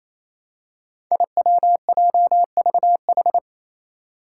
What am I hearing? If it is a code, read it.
Morse code, heard as a keyed tone, about 28 wpm, 700 Hz. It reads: IWJVH
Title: Five Challenging Letter-Number Combinations 28wpm